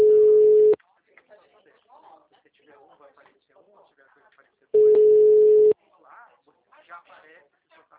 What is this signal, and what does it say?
Ringback tone (Europe, 425 Hz), with speech.